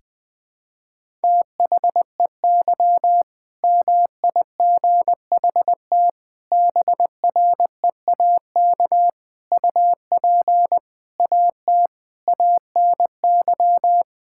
Morse code, heard as a keyed tone, about 21 wpm, 700 Hz. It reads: THEY MIGHT BREAK UP AT ANY
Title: Sentences from Top 1000 Words - Review Part 6 - 20wpm